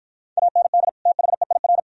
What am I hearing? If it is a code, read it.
Morse code, heard as a keyed tone, about 51 wpm, 700 Hz. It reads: AND THEIR